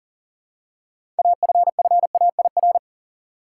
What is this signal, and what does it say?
Morse code, heard as a keyed tone, about 39 wpm, 700 Hz. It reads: AFFAIR
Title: Six Letter Words - 40wpm